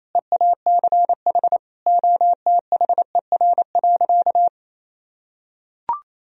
Morse code, heard as a keyed tone, about 28 wpm, 700 Hz. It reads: EACH OTHER.
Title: Sentences from Top 1000 Words - Review Part 1 - 28wpm